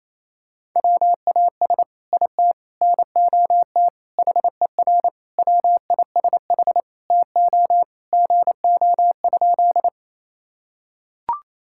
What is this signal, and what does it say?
Morse code, heard as a keyed tone, about 27 wpm, 700 Hz. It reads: WAS IT NOT HER WISH TO GO?
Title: Sentences from Top 1600 Words 28wpm